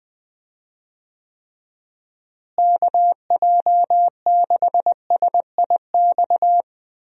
Morse code, heard as a keyed tone, about 21 wpm, 700 Hz. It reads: KJ6SIX
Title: USA Call Signs - 6 Characters - Repeated 2x - 20wpm